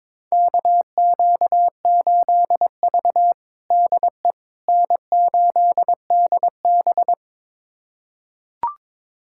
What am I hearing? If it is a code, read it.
Morse code, heard as a keyed tone, about 22 wpm, 700 Hz. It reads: KQ8V DE N8DB
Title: USA Callsign DE Callsign - 4 Characters - 22wpm